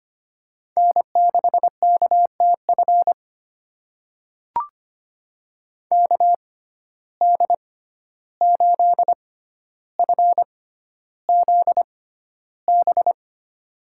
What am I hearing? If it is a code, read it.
Morse code, heard as a keyed tone, about 25 wpm, spaced as for 12 wpm, 700 Hz. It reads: N6KTF KD8FZB
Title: USA Call Signs - 6x Character Spacing - 25wpm